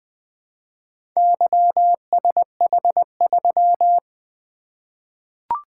Morse code, heard as a keyed tone, about 20 wpm, 700 Hz. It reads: YSH3
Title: Four Letter-Number Combinations - 20wpm